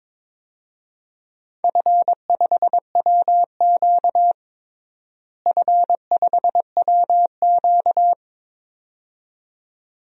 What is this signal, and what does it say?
Morse code, heard as a keyed tone, about 22 wpm, 700 Hz. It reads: F5WQ F5WQ
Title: DXCC-100 Call Signs - 4 Characters - Repeated 2x - 22wpm